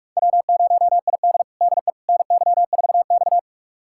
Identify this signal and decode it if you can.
Morse code, heard as a keyed tone, about 41 wpm, 700 Hz. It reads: W0ID DE NY4X
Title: USA Callsign DE Callsign - 4 Characters - 45wpm